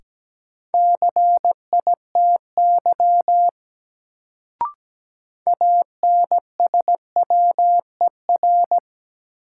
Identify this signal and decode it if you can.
Morse code, heard as a keyed tone, about 17 wpm, 700 Hz. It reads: CITY ANSWER
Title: Sentences from Top 300 Words - New Word Drill - 17wpm